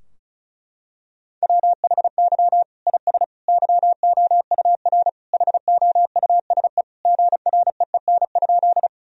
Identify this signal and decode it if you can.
Morse code, heard as a keyed tone, about 34 wpm, 700 Hz. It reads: WHY IS YOUR HOUSE GREEN?
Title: Sentences from Top 1000 Words - Review Part 2 - 35wpm